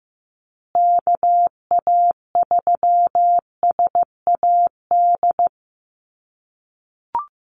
Morse code, heard as a keyed tone, about 15 wpm, 700 Hz. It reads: KA3SAD
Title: USA Call Signs Repeated 2x 15wpm